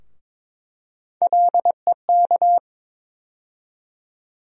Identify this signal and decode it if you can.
Morse code, heard as a keyed tone, about 22 wpm, 700 Hz. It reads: LEK
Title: USA Call Sign Postfixes - 3 Letters 22wpm